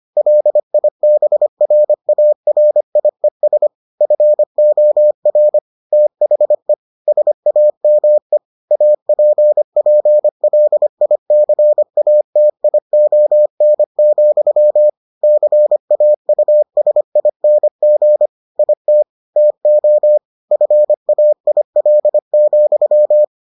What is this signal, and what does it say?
Morse code, heard as a keyed tone, about 25 wpm, 600 Hz. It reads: LIBRARIES FOR THE SAME APPLICATION, CAUSING IT TO FAIL,